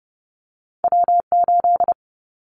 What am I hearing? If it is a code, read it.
Morse code, heard as a keyed tone, about 30 wpm, 700 Hz. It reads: W8
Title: Callsign Prefixes for Top 25 DXCC - Repeated 3x - 30wpm